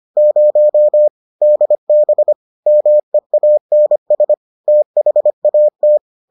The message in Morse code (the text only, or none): 0 DB MEANS THAT